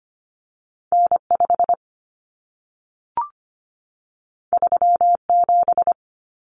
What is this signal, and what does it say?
Morse code, heard as a keyed tone, about 25 wpm, 700 Hz. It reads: N5 37
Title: Common Cut Numbers - 2-Digit 25wpm